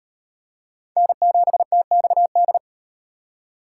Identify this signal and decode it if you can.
Morse code, heard as a keyed tone, about 35 wpm, 700 Hz. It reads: N7TXD